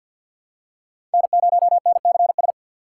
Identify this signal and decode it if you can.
Morse code, heard as a keyed tone, about 52 wpm, 700 Hz. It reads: N0NCS
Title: USA Call Signs - 5 Characters - 2x Character Spacing - 50wpm